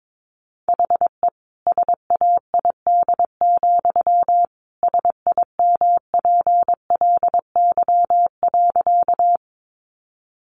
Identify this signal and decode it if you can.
Morse code, heard as a keyed tone, about 22 wpm, 700 Hz. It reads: HE SAID, SIMPLY.
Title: Sentences from Top 600 Words 22wpm